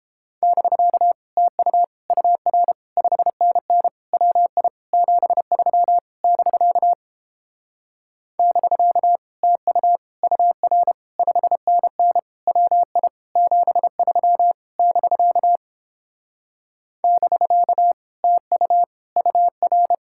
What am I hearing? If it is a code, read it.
Morse code, heard as a keyed tone, about 29 wpm, 700 Hz. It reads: <BK> TU UR 5NN WI 73 <BK> <BK> TU UR 5NN WI 73 <BK> <BK> TU UR